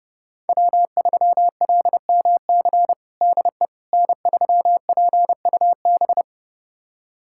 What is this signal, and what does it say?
Morse code, heard as a keyed tone, about 30 wpm, 700 Hz. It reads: W3LMC DE N3PUB
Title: USA Callsign DE Callsign - 5 Characters - 30wpm